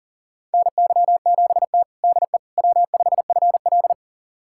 Morse code, heard as a keyed tone, about 40 wpm, 700 Hz. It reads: NY7T DE W5FL
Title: USA Callsign DE Callsign - 4 Characters - 40wpm